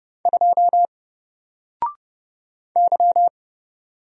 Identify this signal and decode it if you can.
Morse code, heard as a keyed tone, about 31 wpm, 700 Hz. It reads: 2 Y
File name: Single-Letter-Number-30wpm